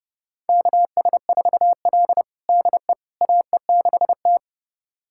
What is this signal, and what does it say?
Morse code, heard as a keyed tone, about 30 wpm, 700 Hz. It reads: KS4L DE AE6T
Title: USA Callsign DE Callsign - 4 Characters - 30wpm